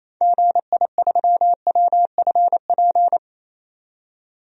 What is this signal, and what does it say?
Morse code, heard as a keyed tone, about 28 wpm, 700 Hz. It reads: GI3WFP
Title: DXCC-100 Call Signs - 6 Characters - 28wpm